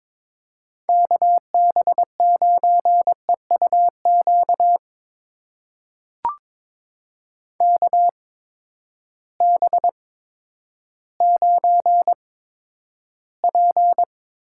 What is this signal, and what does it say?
Morse code, heard as a keyed tone, about 22 wpm, spaced as for 11 wpm, 700 Hz. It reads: KB9EUQ KB9P
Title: USA Call Signs - 6 Characters - 8x Character Spacing - 22wpm